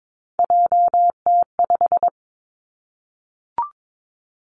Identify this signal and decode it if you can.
Morse code, heard as a keyed tone, about 22 wpm, 700 Hz. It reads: JT5
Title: Three Challenging Letter-Number Combinations 22wpm